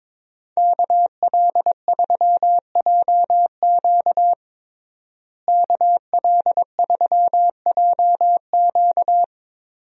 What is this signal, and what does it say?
Morse code, heard as a keyed tone, about 21 wpm, 700 Hz. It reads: KL3JQ KL3JQ
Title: USA Call Signs - 5 Characters - Repeated 2x - 22wpm